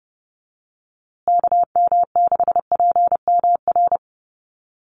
Morse code, heard as a keyed tone, about 30 wpm, 700 Hz. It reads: KM6PMR
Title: USA Call Signs Repeated 2x 30wpm